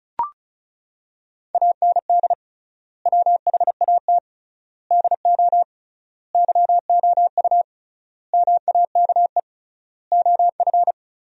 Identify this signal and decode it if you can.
Morse code, heard as a keyed tone, about 36 wpm, 700 Hz. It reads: AND WHAT DO YOU MAKE OF
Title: 3x Word-Spacing - Sentences from Top 100 Words - 35wpm